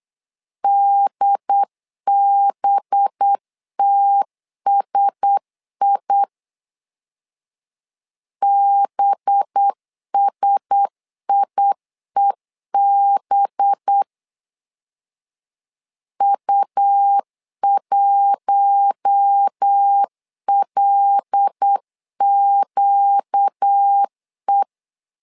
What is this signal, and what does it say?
Morse code, heard as a keyed tone, about 8 wpm, 800 Hz. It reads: DBTSI BSIEB U1LQE
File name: LEZIONI / 5 LBVC / 5 LB 8wpm 2816